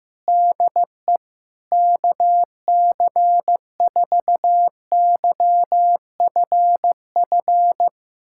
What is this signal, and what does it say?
Morse code, heard as a keyed tone, about 16 wpm, 700 Hz. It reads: DE KC4YFF
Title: USA Callsign DE Callsign - 6 Characters - 15wpm